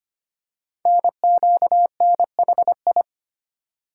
Morse code, heard as a keyed tone, about 25 wpm, 700 Hz. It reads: NQNHI
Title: Code Groups - 5 Letters - 25wpm